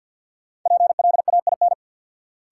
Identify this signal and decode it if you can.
Morse code, heard as a keyed tone, about 47 wpm, 700 Hz. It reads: PLAIN
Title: Sentences from Top 400 Words - New Word Drill - 50wpm